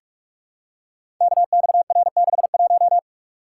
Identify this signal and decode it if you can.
Morse code, heard as a keyed tone, about 44 wpm, 690 Hz. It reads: KXR61
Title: Code Groups - 5 Letter:Numbers - 45wpm